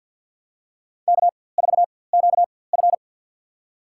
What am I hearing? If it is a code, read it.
Morse code, heard as a keyed tone, about 48 wpm, spaced as for 23 wpm, 700 Hz. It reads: K4QF
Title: USA Call Signs - 4x Character Spacing - 50wpm